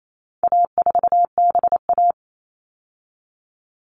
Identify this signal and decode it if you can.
Morse code, heard as a keyed tone, about 28 wpm, 700 Hz. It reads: A4BA
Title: All Cut Numbers - 4-Digit 28wpm